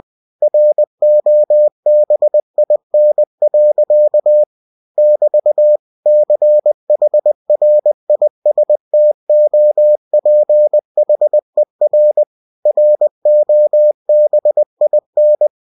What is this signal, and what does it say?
Morse code, heard as a keyed tone, about 20 wpm, 600 Hz. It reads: ROBIN. <BT> CHRISTOPHER ROBIN
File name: cw-books / winnie-the-pooh / winnie-20wpm-page-0017